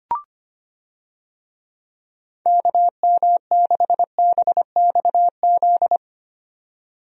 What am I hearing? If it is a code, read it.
Morse code, heard as a keyed tone, about 25 wpm, 700 Hz. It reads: KM6BXZ
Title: USA Call Signs - 6 Characters - 25wpm